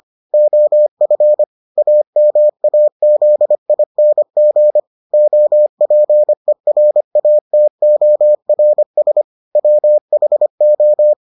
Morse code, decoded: OF AMAZING OPERATORS WHO

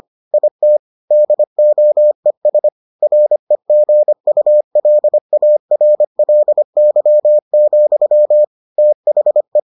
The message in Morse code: IT DOES REGULARLY, THE

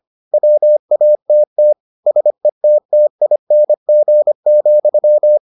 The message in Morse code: WATT SETTING,

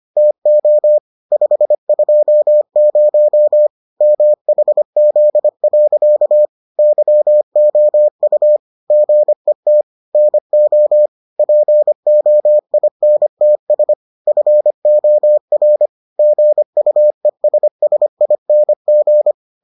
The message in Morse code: TO 520 MHZ. YOU GET NO POINTS FOR GUESSING

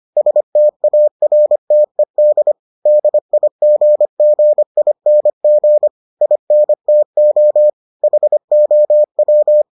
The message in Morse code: STARTED DIGGING INTO HOW

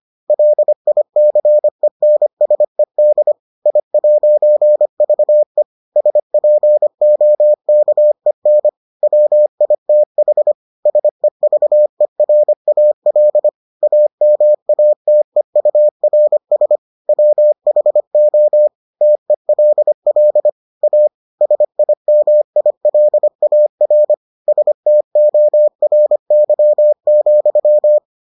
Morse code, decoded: LICENSED I'VE SPOKEN WITH SEVERAL AMATEURS WHO TELL A SIMILAR STORY,

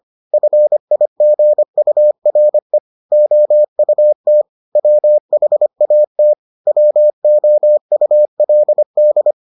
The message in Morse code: FIGURE OUT WHAT WOULD

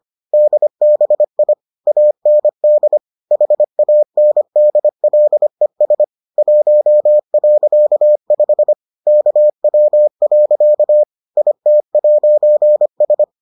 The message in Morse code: DBI AND HANDLES 1.5 KW. IT'S

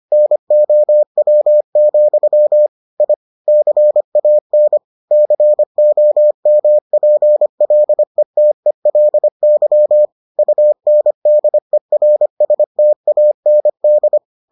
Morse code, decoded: NOW, I CAN COMPLETELY UNDERSTAND

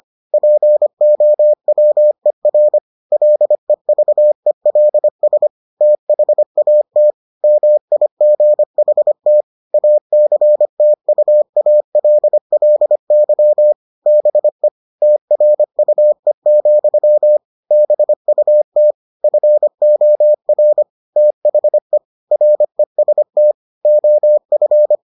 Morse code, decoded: POWER LEVELS THAT MIGHT ACTUALLY BE TRUE, BUT FOR THE REST OF

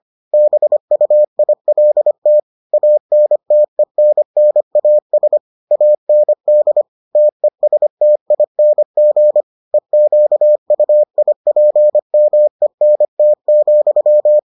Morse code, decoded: BUILT ANTENNAS AND TESTING EQUIPMENT,

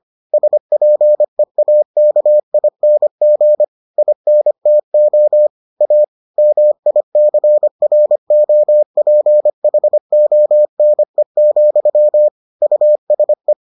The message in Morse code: SPEAKING INTO A MICROPHONE, USE

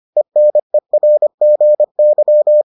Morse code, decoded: ENERGY

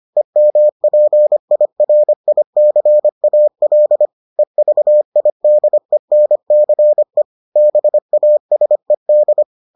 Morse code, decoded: EMPIRICAL EVIDENCE BASED

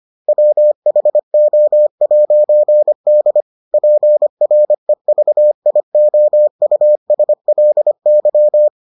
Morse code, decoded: WHO'D PREVIOUSLY

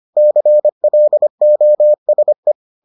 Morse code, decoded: CLOSE

